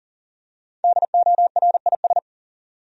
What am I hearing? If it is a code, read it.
Morse code, heard as a keyed tone, about 40 wpm, 700 Hz. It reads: DORIS
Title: USA Common Names - Max 5 Letters - 40wpm